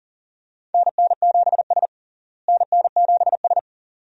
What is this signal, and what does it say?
Morse code, heard as a keyed tone, about 40 wpm, 700 Hz. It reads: NN7S NN7S
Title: USA Call Signs - 4 Characters - Repeated 2x - 40wpm